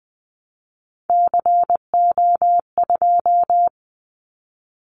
Morse code, heard as a keyed tone, about 20 wpm, 700 Hz. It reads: CO2
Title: Callsign Prefixes for Top 100 DXCC - 20wpm